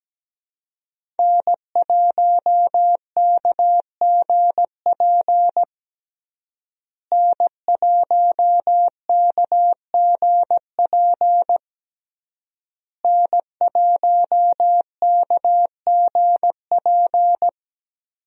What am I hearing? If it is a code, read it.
Morse code, heard as a keyed tone, about 17 wpm, 700 Hz. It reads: N1KGP N1KGP N1KGP
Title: USA Call Signs - 5 Characters - Repeated 3x - 17wpm